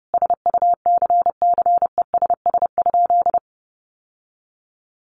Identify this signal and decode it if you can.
Morse code, heard as a keyed tone, about 30 wpm, 700 Hz. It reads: SUCCESS?
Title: Sentences from Top 900 Words 30wpm